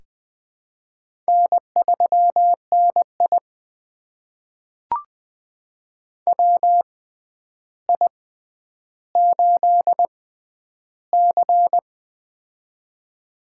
Morse code, heard as a keyed tone, about 20 wpm, spaced as for 10 wpm, 700 Hz. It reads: N3NI WI8C
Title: USA Call Signs - 4 Characters - 6x Character Spacing - 20wpm